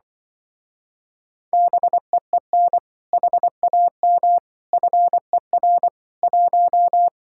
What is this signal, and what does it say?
Morse code, heard as a keyed tone, about 24 wpm, 700 Hz. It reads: BEEN HAM FER 1
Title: Speed-Racing Common QSO Phrases - 20wpm